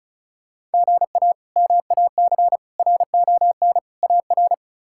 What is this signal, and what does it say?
Morse code, heard as a keyed tone, about 36 wpm, 700 Hz. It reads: GA MAC RON AR
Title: K1USN SST QSO Practice - 35wpm